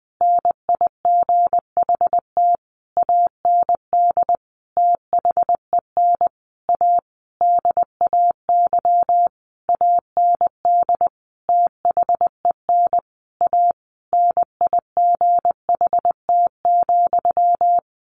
Morse code, heard as a keyed tone, about 20 wpm, 700 Hz. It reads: NIGHT AND THEN A DAY AND THEN A NIGHT,
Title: A Sound Of Thunder 20wpm